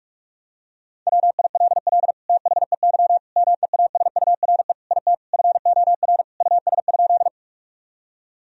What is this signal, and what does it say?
Morse code, heard as a keyed tone, about 46 wpm, 700 Hz. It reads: WILL THEY MEASURE IT FOR US?
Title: Sentences from Top 1000 Words - Review Part 2 - 45wpm